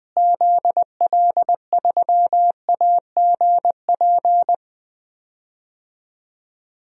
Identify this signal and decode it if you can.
Morse code, heard as a keyed tone, about 20 wpm, 700 Hz. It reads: ZL3AGP
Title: DXCC-100 Call Signs - 6 Characters - Repeated 2x - 20wpm